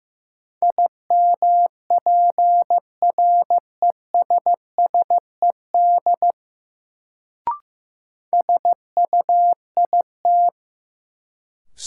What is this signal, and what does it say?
Morse code, heard as a keyed tone, about 15 wpm, 700 Hz. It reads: IMPRESSED SUIT
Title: Sentences from Top 1600 Words - New Word Drill - 15wpm